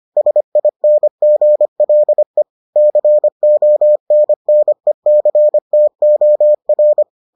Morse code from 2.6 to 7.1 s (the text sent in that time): CONNECTOR